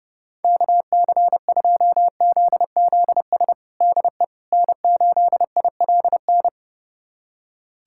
Morse code, heard as a keyed tone, about 30 wpm, 700 Hz. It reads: KC2ZZS DE N8ILN
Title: USA Callsign DE Callsign 30wpm